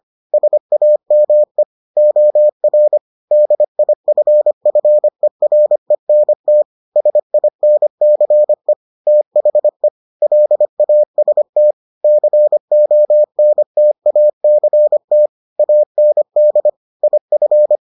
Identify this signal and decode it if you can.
Morse code, heard as a keyed tone, about 25 wpm, 600 Hz. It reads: SAME OR DIFFERENT SINCE THE LAST CONTACT AND IF